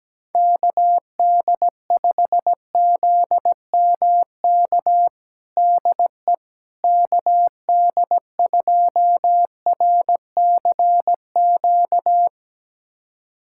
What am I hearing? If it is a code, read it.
Morse code, heard as a keyed tone, about 17 wpm, 700 Hz. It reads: KD5ZMK DE KD2RCQ
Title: USA Callsign DE Callsign - 6 Characters - 17wpm